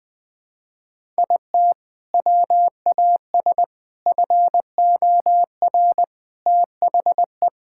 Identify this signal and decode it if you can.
Morse code, heard as a keyed tone, about 19 wpm, 700 Hz. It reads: IT WAS FOR THE
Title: Sentences from Top 1000 Words - Review Part 6 - 20wpm